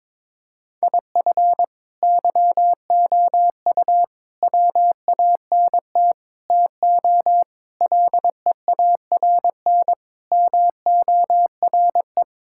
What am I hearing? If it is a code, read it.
Morse code, heard as a keyed tone, about 22 wpm, 700 Hz. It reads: IF YOU WANT TO LEARN MORE
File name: Sentences with 7-Word Phrases - 22wpm